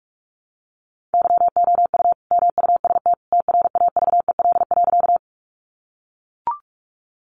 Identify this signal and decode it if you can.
Morse code, heard as a keyed tone, about 44 wpm, 700 Hz. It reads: YOU MUST TRAVEL.
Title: Sentences from Top 400 Words 45wpm